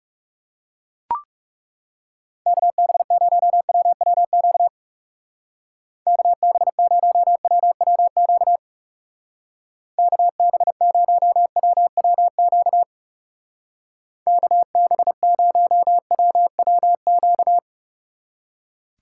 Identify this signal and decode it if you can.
Morse code, heard as a keyed tone, about 36 wpm, 700 Hz. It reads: KB0WWQ KB0WWQ KB0WWQ KB0WWQ
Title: Speed Racing - USA Call Signs 30wpm